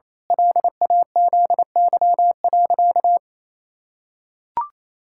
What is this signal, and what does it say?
Morse code, heard as a keyed tone, about 28 wpm, 700 Hz. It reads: LAZY.
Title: Pangram Challenge v2 - 28wpm